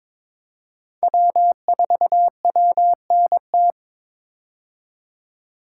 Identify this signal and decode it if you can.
Morse code, heard as a keyed tone, about 22 wpm, 700 Hz. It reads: W4WNT
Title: K1USN SST QSO Practice - 22wpm